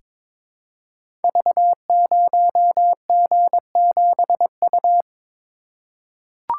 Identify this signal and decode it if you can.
Morse code, heard as a keyed tone, about 22 wpm, 700 Hz. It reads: V0G7U
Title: Code Groups - 5 Letter:Numbers - 22wpm